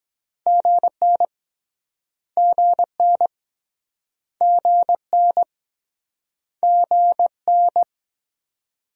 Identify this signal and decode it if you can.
Morse code, heard as a keyed tone, about 24 wpm, 700 Hz. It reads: GN GN GN GN